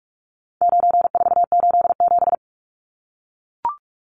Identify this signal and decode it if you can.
Morse code, heard as a keyed tone, about 44 wpm, 700 Hz. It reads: 9487
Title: Numbers - 4-Digit 45wpm